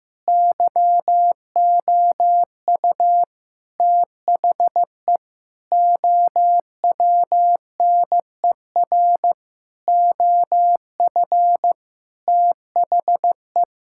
Morse code, heard as a keyed tone, about 15 wpm, 700 Hz. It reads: YOU THE OWNER OF THE
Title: Sentences from Top 1600 Words 15wpm